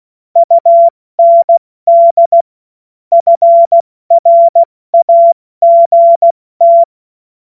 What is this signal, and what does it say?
Morse code, heard as a keyed tone, about 16 wpm, spaced as for 14 wpm, 680 Hz. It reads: UND FRAGT
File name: Morsekurs / Witze / Witz 24. Januar 2023 70 F80